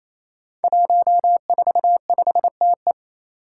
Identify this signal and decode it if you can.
Morse code, heard as a keyed tone, about 27 wpm, 700 Hz. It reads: 145TE
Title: Five Challenging Letter-Number Combinations 28wpm